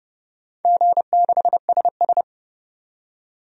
Morse code, heard as a keyed tone, about 31 wpm, 700 Hz. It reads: G6SS
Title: DXCC-100 Call Signs - 4 Characters - 30wpm